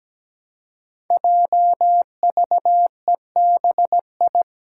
Morse code, heard as a keyed tone, about 17 wpm, 700 Hz. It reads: JVEBI
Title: Five Challenging Letter-Number Combinations 17wpm